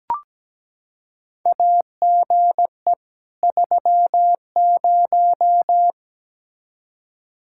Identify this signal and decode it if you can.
Morse code, heard as a keyed tone, about 17 wpm, 700 Hz. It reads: AGE 30
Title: Practice QSO - Exchange 3 - More Info - 17wpm